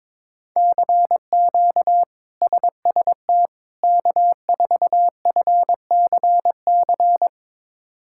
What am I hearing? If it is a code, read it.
Morse code, heard as a keyed tone, about 23 wpm, 700 Hz. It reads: CQ SST K4FCC
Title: K1USN SST QSO Practice - 22wpm